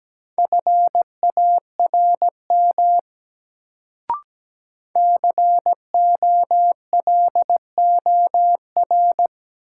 Morse code, heard as a keyed tone, about 17 wpm, 700 Hz. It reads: FARM COLOR
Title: Sentences from Top 1000 Words - Review Part 2 - New Word Drill - 17wpm